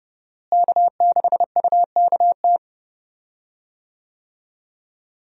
Morse code, heard as a keyed tone, about 30 wpm, 700 Hz. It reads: K6UKT
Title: USA Call Signs - 5 Characters - Repeated 3x - 30wpm